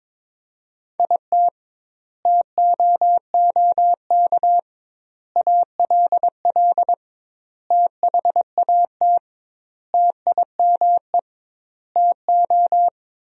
Morse code, heard as a keyed tone, about 22 wpm, 700 Hz. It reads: IT TOOK ALL THAT TIME TO
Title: 2x Word-Spacing - Sentences from Top 300 Words - 22wpm